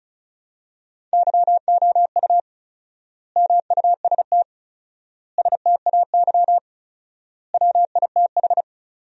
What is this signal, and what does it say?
Morse code, heard as a keyed tone, about 36 wpm, 700 Hz. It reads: YOU MUST STAY WITH
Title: 4x Word-Spacing - Sentences from Top 500 Words - 35wpm